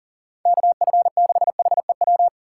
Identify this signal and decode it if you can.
Morse code, heard as a keyed tone, about 38 wpm, 700 Hz. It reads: KF6HEW
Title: USA Call Signs - 3x Character Spacing - 40wpm